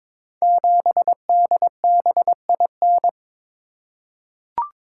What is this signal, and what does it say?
Morse code, heard as a keyed tone, about 22 wpm, 700 Hz. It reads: 7DBIN
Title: Code Groups - 5 Letter:Numbers - 22wpm